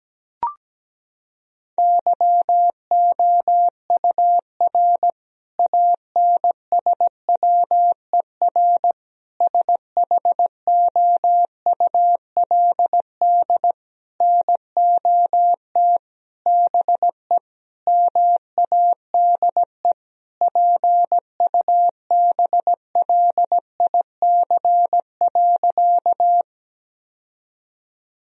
Morse code, heard as a keyed tone, about 17 wpm, 700 Hz. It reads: YOUR ANSWER SHOULD NOT BE MADE PUBLIC.
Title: Sentences from Top 1000 Words - Review Part 3 - 17wpm